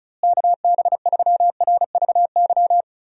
Morse code, heard as a keyed tone, about 36 wpm, 700 Hz. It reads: KB3RVY
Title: USA Call Signs - 6 Characters - Repeated 3x - 35wpm